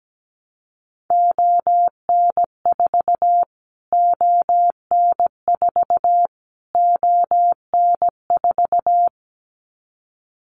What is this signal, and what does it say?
Morse code, heard as a keyed tone, about 17 wpm, 700 Hz. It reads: ON4 ON4 ON4
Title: Callsign Prefixes for Top 25 DXCC - Repeated 3x - 17wpm